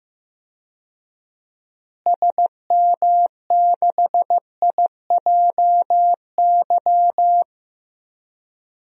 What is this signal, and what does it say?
Morse code, heard as a keyed tone, about 15 wpm, 700 Hz. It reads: SM6IJY
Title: DXCC-100 Call Signs - 6 Characters - Repeated 2x - 15wpm